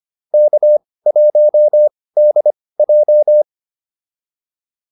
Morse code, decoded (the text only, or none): K1DJ